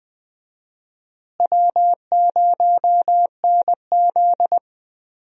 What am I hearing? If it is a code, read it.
Morse code, heard as a keyed tone, about 20 wpm, 700 Hz. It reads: W0NZ
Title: USA Call Signs - 4 Characters - Repeated 3x - 20wpm